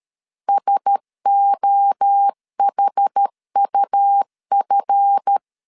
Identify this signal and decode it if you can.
Morse code, heard as a keyed tone, about 13 wpm, 800 Hz. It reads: SOHUF
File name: CW / DET / 1 9 1500C S C 12wpm 3811